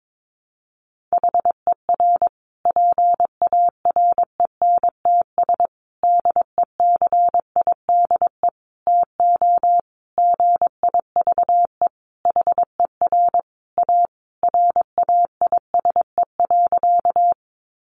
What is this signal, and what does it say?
Morse code, heard as a keyed tone, about 22 wpm, 700 Hz. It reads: HER PARENTS DECIDE TO GIVE HER A RAISE.
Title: Easy Phrases 22wpm